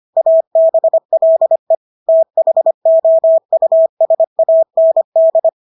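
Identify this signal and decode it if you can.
Morse code, heard as a keyed tone, about 25 wpm, 650 Hz. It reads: ABLE THOUSAND
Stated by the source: Morse practice of English words at 25 wpm from CW med Gnister.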